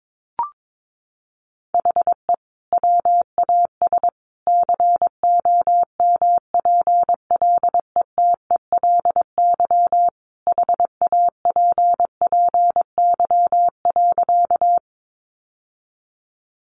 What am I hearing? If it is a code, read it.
Morse code, heard as a keyed tone, about 22 wpm, 700 Hz. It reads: HE WAS COMPLETELY HAPPY.
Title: Sentences from Top 1100 Words 22wpm